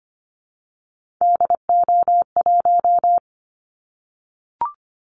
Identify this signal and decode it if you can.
Morse code, heard as a keyed tone, about 25 wpm, 700 Hz. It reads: DO1
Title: Callsign Prefixes for Top 25 DXCC - 25wpm